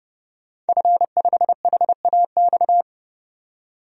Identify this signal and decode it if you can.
Morse code, heard as a keyed tone, about 30 wpm, 700 Hz. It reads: F5HAX
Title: DXCC-100 Call Signs - 5 Characters - 30wpm